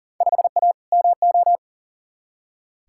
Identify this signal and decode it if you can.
Morse code, heard as a keyed tone, about 40 wpm, 700 Hz. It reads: 5A MO
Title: Field Day Exchanges - 40wpm